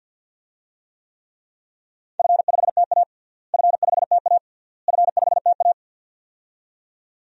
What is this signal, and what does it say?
Morse code, heard as a keyed tone, about 47 wpm, 700 Hz. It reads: F5TA F5TA F5TA
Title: DXCC-100 Call Signs - 4 Characters - Repeated 3x - 50wpm